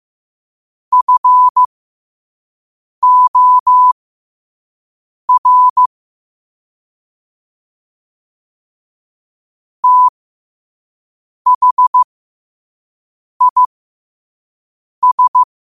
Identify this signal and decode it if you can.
Morse code, heard as a keyed tone, about 15 wpm, spaced as for 5 wpm, 1000 Hz. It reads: FOR THIS
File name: qotdfiles / 5wpm